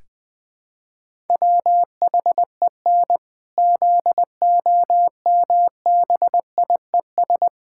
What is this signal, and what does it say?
Morse code, heard as a keyed tone, about 20 wpm, 700 Hz. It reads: WHEN ZOMBIES
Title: Pangram Challenge v2 - 20wpm